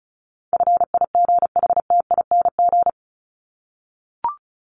Morse code, heard as a keyed tone, about 34 wpm, 700 Hz. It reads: FIGHTING
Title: ING Suffix - Top 100 Words with Base Words 35wpm